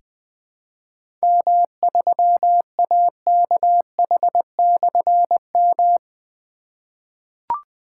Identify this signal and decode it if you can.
Morse code, heard as a keyed tone, about 20 wpm, 700 Hz. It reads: M3AKH/M
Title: Difficult Call Signs 20wpm